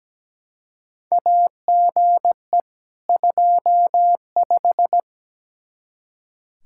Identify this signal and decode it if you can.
Morse code, heard as a keyed tone, about 17 wpm, 700 Hz. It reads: AGE 25
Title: Practice QSO - Exchange 3 - More Info - 17wpm